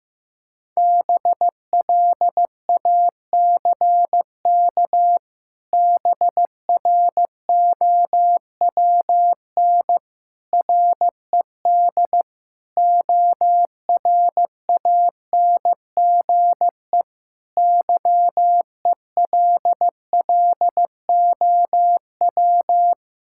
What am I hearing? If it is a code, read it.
Morse code, heard as a keyed tone, about 15 wpm, 700 Hz. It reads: BLACK BROWN RED ORANGE YELLOW
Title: Pangram Challenge v2 - 15wpm